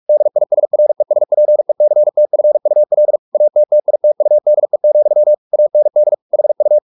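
Morse code, heard as a keyed tone, about 44 wpm, 600 Hz. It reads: DISRESPECTFUL ATTITUDE, AND HU
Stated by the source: Morse practice at 45 wpm, the novel Never by Ken Follett in English, from CW med Gnister.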